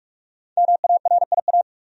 Morse code, heard as a keyed tone, about 42 wpm, 700 Hz. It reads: MARIA
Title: USA Common Names - Max 5 Letters - 45wpm